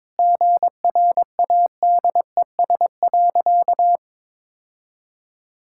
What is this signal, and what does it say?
Morse code, heard as a keyed tone, about 22 wpm, 700 Hz. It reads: GRADES.
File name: practice / 22wpm / Sentences with 4-Word Phrases - 22wpm